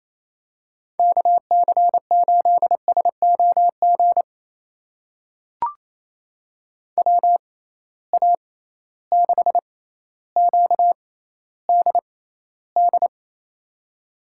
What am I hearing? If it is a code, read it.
Morse code, heard as a keyed tone, about 28 wpm, spaced as for 14 wpm, 700 Hz. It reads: KC8SOG WA6QDD
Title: USA Call Signs - 6 Characters - 6x Character Spacing - 28wpm